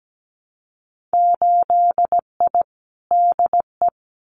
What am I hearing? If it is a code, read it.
Morse code, heard as a keyed tone, about 17 wpm, 700 Hz. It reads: 8I DE
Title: Winter Field Day Exchanges Encore 17wpm